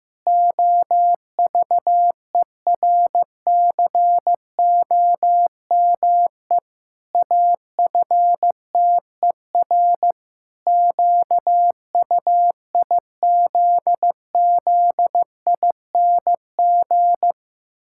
Morse code, heard as a keyed tone, about 15 wpm, 700 Hz. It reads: OVERCOME AFTER QUIZZING
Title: Pangram Challenge v2 - 15wpm